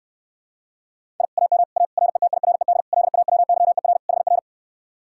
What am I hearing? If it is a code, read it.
Morse code, heard as a keyed tone, about 69 wpm, 700 Hz. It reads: I AM A RETIRED LAWYER HR